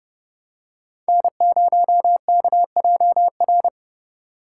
Morse code, heard as a keyed tone, about 31 wpm, 700 Hz. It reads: N0KJR